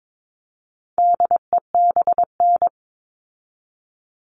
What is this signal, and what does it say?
Morse code, heard as a keyed tone, about 22 wpm, 700 Hz. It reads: DEBN
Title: All Cut Numbers - 4-Digit 22wpm